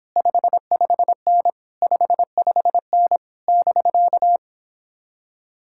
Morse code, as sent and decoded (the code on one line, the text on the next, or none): ..... ..... -. / ..... ..... -. / -...-.-
55N 55N <BK>